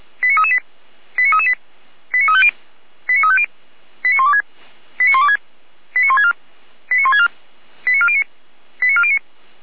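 Five-tone selective call (ZVEI), heard as quick runs of five tones, tones of about 70 ms. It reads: 89338 89338 89350 89350 89126 89126 89164 89164 89409 89409
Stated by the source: ZVEI five-tone call received by SDR.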